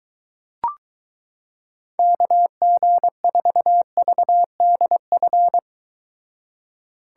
Morse code, heard as a keyed tone, about 23 wpm, 700 Hz. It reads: KG4VDF